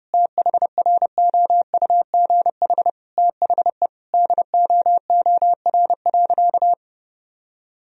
Morse code, heard as a keyed tone, about 30 wpm, 700 Hz. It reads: THROUGH THE DOOR.
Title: Sentences from Top 1000 Words - Review Part 1 - 30wpm